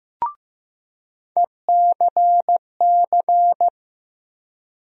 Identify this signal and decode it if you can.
Morse code, heard as a keyed tone, about 15 wpm, 700 Hz. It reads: ECC
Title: USA Call Sign Postfixes - 3 Letters 15wpm